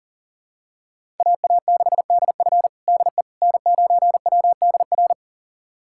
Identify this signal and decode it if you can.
Morse code, heard as a keyed tone, about 40 wpm, 700 Hz. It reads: AA6DF DE N9WDR
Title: USA Callsign DE Callsign - 5 Characters - 40wpm